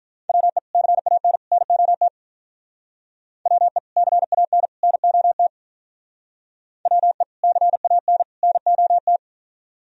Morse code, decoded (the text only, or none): WE CAN NOT WE CAN NOT WE CAN NOT